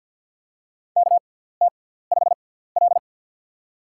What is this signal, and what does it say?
Morse code, heard as a keyed tone, about 52 wpm, spaced as for 17 wpm, 700 Hz. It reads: KT5L
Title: USA Call Signs - 6x Character Spacing - 50wpm